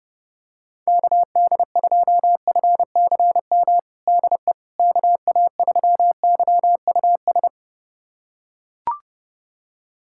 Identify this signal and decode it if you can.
Morse code, heard as a keyed tone, about 30 wpm, 700 Hz. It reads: KD2FCM DE KA3YUS
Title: USA Callsign DE Callsign - 6 Characters - 30wpm